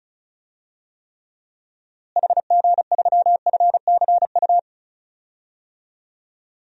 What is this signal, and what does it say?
Morse code, heard as a keyed tone, about 36 wpm, 700 Hz. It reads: HG3FCU
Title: DXCC-100 Call Signs - 6 Characters - Repeated 3x - 35wpm